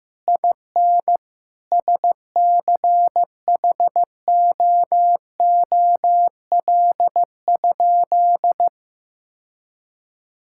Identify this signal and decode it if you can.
Morse code, heard as a keyed tone, about 15 wpm, 700 Hz. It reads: IN SCHOOL?
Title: Sentences from Top 1000 Words - Review Part 5 - 15wpm